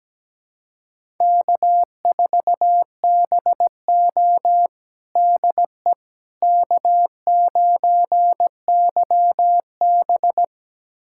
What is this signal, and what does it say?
Morse code, heard as a keyed tone, about 17 wpm, 700 Hz. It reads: K4BO DE K9YB
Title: USA Callsign DE Callsign - 4 Characters - 17wpm